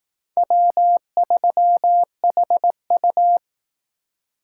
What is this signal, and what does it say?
Morse code, heard as a keyed tone, about 18 wpm, 700 Hz. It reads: W3HU
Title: Speed Racing - USA 1x2 Call Signs 15wpm